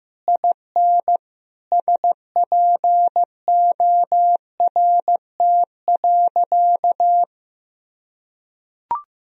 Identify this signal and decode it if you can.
Morse code, heard as a keyed tone, about 15 wpm, 700 Hz. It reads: IN SPORT.
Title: Sentences from Top 1000 Words - Review Part 4 - 15wpm